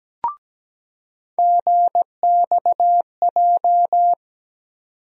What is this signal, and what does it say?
Morse code, heard as a keyed tone, about 17 wpm, 700 Hz. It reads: GXJ
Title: USA Call Sign Postfixes - 3 Letters 17wpm